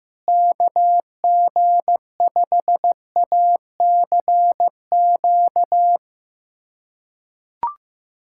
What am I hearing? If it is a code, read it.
Morse code, heard as a keyed tone, about 15 wpm, 700 Hz. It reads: KG5ACQ
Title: USA Callsign DE Callsign - 6 Characters - 15wpm